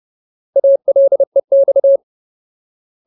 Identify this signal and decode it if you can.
Morse code, heard as a keyed tone, about 30 wpm, 550 Hz. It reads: ALEX
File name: Ultimate Word List / 30WPM Ultimate